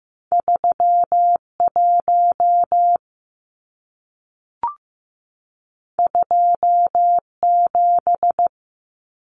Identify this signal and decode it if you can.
Morse code, heard as a keyed tone, about 15 wpm, 700 Hz. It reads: 31 27
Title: Numbers - 2-Digit 15wpm